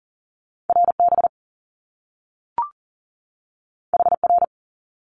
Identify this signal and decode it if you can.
Morse code, heard as a keyed tone, about 38 wpm, 700 Hz. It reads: RB HR
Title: Top Two Letter Combinations in Words 40wpm